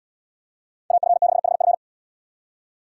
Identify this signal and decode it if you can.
Morse code, heard as a keyed tone, about 73 wpm, 700 Hz. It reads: AF6HF